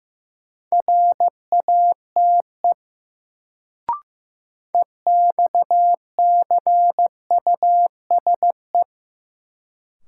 Morse code, heard as a keyed tone, about 15 wpm, 700 Hz. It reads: RATE EXCUSE
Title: Sentences from Top 700 Words - New Word Drill - 15wpm